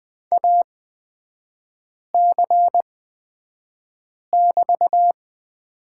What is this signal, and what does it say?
Morse code, heard as a keyed tone, about 20 wpm, spaced as for 5 wpm, 700 Hz. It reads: AC<BT>
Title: 224 Lesson 36e - BT - Three Characters 20wpm